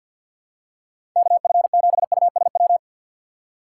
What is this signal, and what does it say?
Morse code, heard as a keyed tone, about 49 wpm, 700 Hz. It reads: KF7USW
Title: USA Call Signs - 6 Characters - 8x Character Spacing - 50wpm